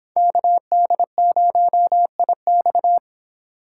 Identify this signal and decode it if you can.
Morse code, heard as a keyed tone, about 26 wpm, 700 Hz. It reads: KD0IX